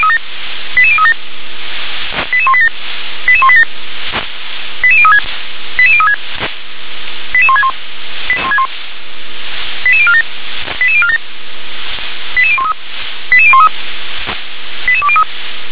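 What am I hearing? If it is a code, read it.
Five-tone selective call (ZVEI), heard as quick runs of five tones, tones of about 70 ms. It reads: 88037 88187 88187 88036 88036 88161 88047 88047 88013 88013 88203
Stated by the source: Five-tone ZVEI selective call received on SDR.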